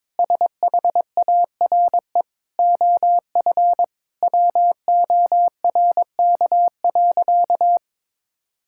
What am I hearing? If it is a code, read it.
Morse code, heard as a keyed tone, about 22 wpm, 700 Hz. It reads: SHARE OF WORK.
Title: Sentences from Top 1000 Words - Review Part 6 - 22wpm